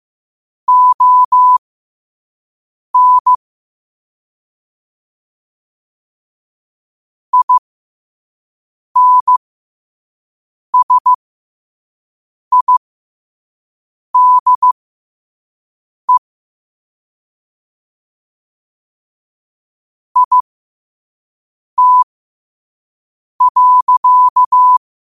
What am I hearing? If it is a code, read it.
Morse code, heard as a keyed tone, about 15 wpm, spaced as for 5 wpm, 1000 Hz. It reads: ON INSIDE IT.
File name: qotdfiles / 5wpm